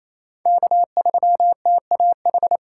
Morse code, heard as a keyed tone, about 28 wpm, 700 Hz. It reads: K3TAH
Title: USA Call Signs - 3x Character Spacing - 28wpm